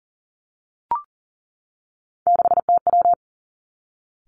Morse code, heard as a keyed tone, about 42 wpm, 690 Hz. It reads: 6TW
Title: Three Challenging Letter-Number Combinations 40wpm